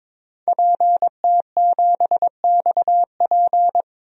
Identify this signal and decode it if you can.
Morse code, heard as a keyed tone, about 21 wpm, 700 Hz. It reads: PT7XP
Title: DXCC-100 Call Signs - 5 Characters - 22wpm